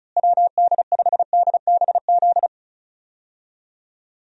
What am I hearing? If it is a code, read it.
Morse code, heard as a keyed tone, about 34 wpm, 700 Hz. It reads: WD5DBZ
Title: USA Call Signs - 6 Characters - Repeated 2x - 35wpm